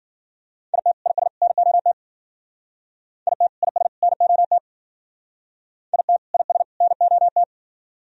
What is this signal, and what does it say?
Morse code, heard as a keyed tone, about 51 wpm, 700 Hz. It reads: IT IS NOT IT IS NOT IT IS NOT